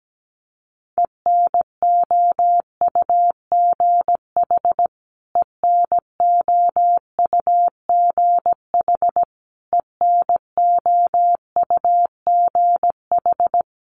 Morse code, heard as a keyed tone, about 17 wpm, 700 Hz. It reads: ENOUGH ENOUGH ENOUGH
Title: Top 500 Words Repeated 17wpm